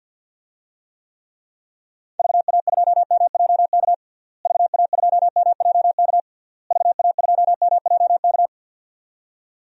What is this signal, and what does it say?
Morse code, heard as a keyed tone, about 51 wpm, 700 Hz. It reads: VA2MJK VA2MJK VA2MJK
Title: DXCC-100 Call Signs - 6 Characters - Repeated 3x - 50wpm